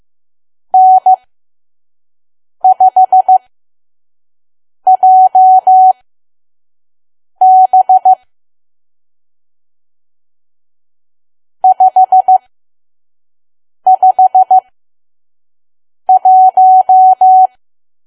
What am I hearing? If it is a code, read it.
Morse code, heard as a keyed tone, about 15 wpm, spaced as for 5 wpm, 750 Hz. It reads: N5JB 551
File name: Archive / 5 WPM files / PM